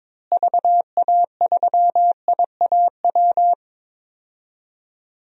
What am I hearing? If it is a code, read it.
Morse code, heard as a keyed tone, about 22 wpm, 700 Hz. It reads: VA3IAW
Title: DXCC-100 Call Signs - 6 Characters - Repeated 3x - 22wpm